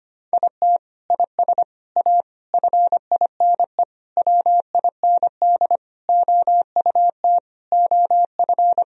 Morse code, heard as a keyed tone, about 25 wpm, 700 Hz. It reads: IT IS A FINE WIND OUT OF
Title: Sentences from Top 1000 Words - Review Part 2 - 25wpm